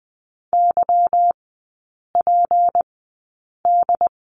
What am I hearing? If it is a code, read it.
Morse code, heard as a keyed tone, about 20 wpm, spaced as for 8 wpm, 700 Hz. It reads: YPD
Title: 205 Lesson 33e - Letter j - Three Characters 20wpm